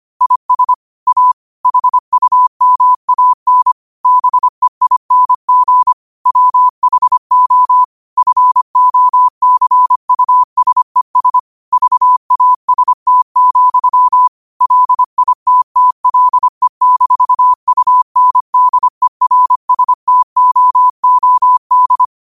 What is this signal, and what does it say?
Morse code, heard as a keyed tone, about 25 wpm, 1000 Hz. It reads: IS A HUMAN BEING WHO FOCUSES VAST, LITTLE-UNDERSTOOD
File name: qotdfiles / 25wpm